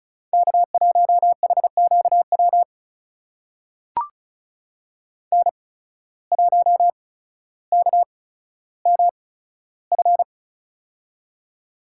Morse code, heard as a keyed tone, about 36 wpm, spaced as for 15 wpm, 700 Hz. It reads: K1HQW N1KMF
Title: USA Call Signs - 5 Characters - 8x Character Spacing - 35wpm